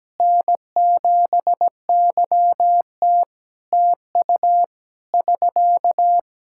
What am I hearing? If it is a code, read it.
Morse code, heard as a keyed tone, about 17 wpm, 700 Hz. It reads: N7YT TU <SK>
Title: Practice QSO - Exchange 4 - Ending - 17wpm